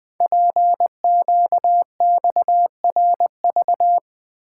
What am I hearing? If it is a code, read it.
Morse code, heard as a keyed tone, about 21 wpm, 700 Hz. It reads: PQXRV
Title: Code Groups - 5 Letters - 20wpm